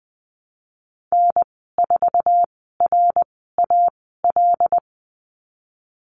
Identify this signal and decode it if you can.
Morse code, heard as a keyed tone, about 20 wpm, spaced as for 14 wpm, 700 Hz. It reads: N4RAL
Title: 076 Lesson 13g - Letter L - Call Signs 20wpm